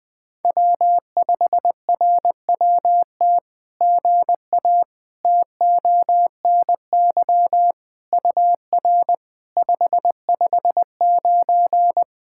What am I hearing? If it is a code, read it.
Morse code, heard as a keyed tone, about 20 wpm, 700 Hz. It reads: W5RWT GA TONY UR 559
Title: SOTA QSO Practice 20wpm